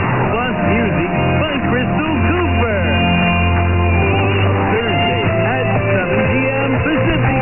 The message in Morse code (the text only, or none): none